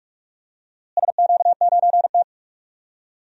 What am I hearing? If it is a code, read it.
Morse code, heard as a keyed tone, about 46 wpm, 700 Hz. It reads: SQ9T
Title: DXCC-100 Call Signs - 4 Characters - Repeated 2x - 45wpm